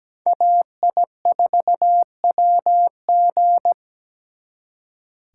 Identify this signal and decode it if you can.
Morse code, heard as a keyed tone, about 20 wpm, 700 Hz. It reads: AI4WG
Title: Speed Racing - USA Call Signs 17wpm